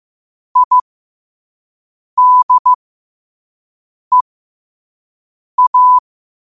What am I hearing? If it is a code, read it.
Morse code, heard as a keyed tone, about 15 wpm, spaced as for 5 wpm, 1000 Hz. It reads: IDEA
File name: qotdfiles / 5wpm